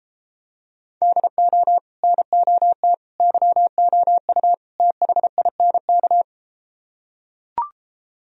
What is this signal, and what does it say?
Morse code, heard as a keyed tone, about 33 wpm, 700 Hz. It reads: DO NOT YOU THINK